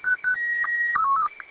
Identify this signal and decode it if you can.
SSTV (slow-scan television), Scottie 1.
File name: w8lap- -combined